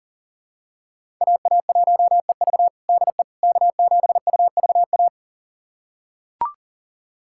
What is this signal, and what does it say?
Morse code, heard as a keyed tone, about 40 wpm, 700 Hz. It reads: AA1EV DE K7UVA
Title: USA Callsign DE Callsign - 5 Characters - 40wpm